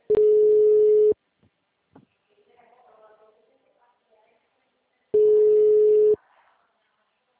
Ringback tone (Europe, 425 Hz).